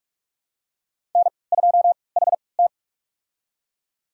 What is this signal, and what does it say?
Morse code, heard as a keyed tone, about 46 wpm, spaced as for 25 wpm, 690 Hz. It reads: N2HT
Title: USA Call Signs - 4 Characters - 3x Character Spacing - 45wpm